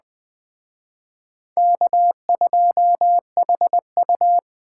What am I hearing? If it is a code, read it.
Morse code, heard as a keyed tone, about 21 wpm, 700 Hz. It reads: K2HU
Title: Speed Racing - USA 1x2 Call Signs 17wpm